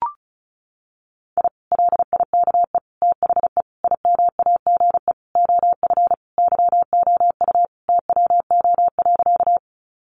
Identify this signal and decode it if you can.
Morse code, heard as a keyed tone, about 34 wpm, 700 Hz. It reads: I LIKE THE IMAGE OF YOU TWO.
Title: Sentences from Top 700 Words 35wpm